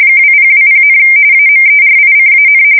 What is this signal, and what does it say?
RTTY, about 45.45 baud. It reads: 2020, AMERICAN
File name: fsk170 / USS-New-Jersey